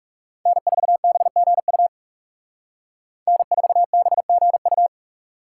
Morse code, heard as a keyed tone, about 36 wpm, 700 Hz. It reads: N4BGU N4BGU
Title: Speed Racing - USA 1x3 Call Signs 30wpm